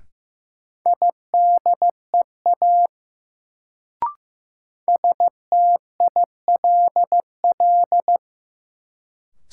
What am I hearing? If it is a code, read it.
Morse code, heard as a keyed tone, about 15 wpm, 700 Hz. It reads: IDEA STILL
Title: Sentences from Top 300 Words - New Word Drill - 15wpm